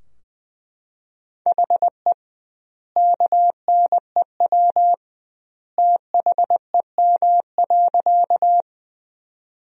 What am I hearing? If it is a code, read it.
Morse code, heard as a keyed tone, about 20 wpm, 700 Hz. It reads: HE KNEW THEM.
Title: 2x Word-Spacing - Sentences from Top 400 Words - 20wpm